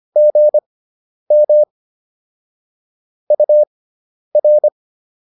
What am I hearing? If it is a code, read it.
Morse code, heard as a keyed tone, about 25 wpm, spaced as for 10 wpm, 600 Hz. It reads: GM UR